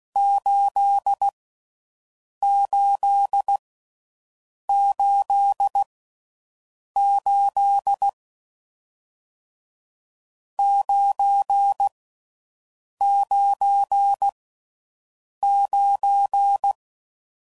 Morse code, heard as a keyed tone, about 16 wpm, 800 Hz. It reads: 8 8 8 8 9 9 9